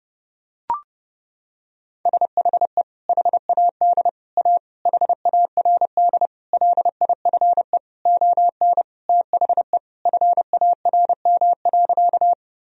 Morse code, heard as a keyed tone, about 30 wpm, 700 Hz. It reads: SHE HAD A HARD LIFE ON THE FARM.
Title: Sentences from Top 1000 Words - Review Part 2 - 30wpm